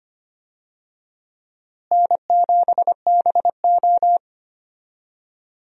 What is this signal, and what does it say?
Morse code, heard as a keyed tone, about 25 wpm, 700 Hz. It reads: N7BO
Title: USA Call Signs - 4 Characters - Repeated 2x - 25wpm